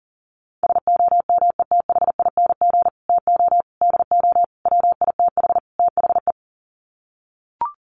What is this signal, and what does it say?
Morse code, heard as a keyed tone, about 40 wpm, 700 Hz. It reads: SOMETHING TO DO WITH THE
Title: Sets of 5 Words Encore 40wpm